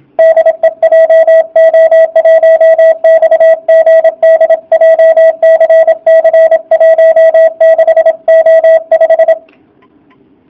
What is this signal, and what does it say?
Morse code, heard as a keyed tone, about 26 wpm, 660 Hz, with radio noise. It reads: DE JO1XGD JCC 16O5
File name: jh8cbh / 2019cw / sp32- w